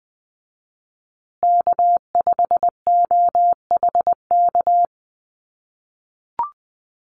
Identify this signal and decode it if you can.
Morse code, heard as a keyed tone, about 20 wpm, 700 Hz. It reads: K5OHK
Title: USA Call Signs Repeated 2x 20wpm